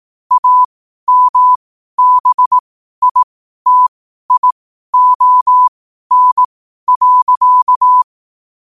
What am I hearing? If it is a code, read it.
Morse code, heard as a keyed tone, about 18 wpm, spaced as for 12 wpm, 1000 Hz. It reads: AMBITION.
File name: qotdfiles / 12wpm